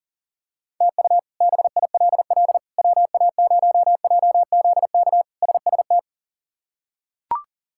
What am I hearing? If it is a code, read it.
Morse code, heard as a keyed tone, about 40 wpm, 700 Hz. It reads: TU BILL WA0JZK SST
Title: K1USN SST QSO Practice - 40wpm